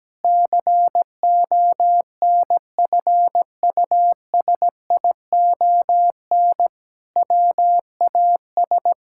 Morse code, heard as a keyed tone, about 17 wpm, 700 Hz. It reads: CONFUSION WAS
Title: Sentences from Top 1600 Words 17wpm